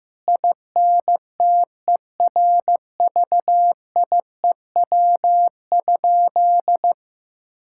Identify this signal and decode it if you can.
Morse code, heard as a keyed tone, about 15 wpm, 700 Hz. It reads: INTERVIEW?
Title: Sentences from Top 1600 Words 15wpm